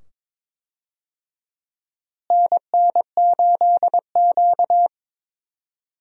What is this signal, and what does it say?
Morse code, heard as a keyed tone, about 22 wpm, 700 Hz. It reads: NN8Q
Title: USA Call Signs - 4 Characters - Repeated 2x - 22wpm